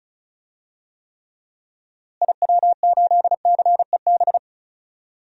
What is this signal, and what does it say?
Morse code, heard as a keyed tone, about 36 wpm, 700 Hz. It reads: IW8CEB
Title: DXCC-100 Call Signs - 6 Characters - Repeated 2x - 35wpm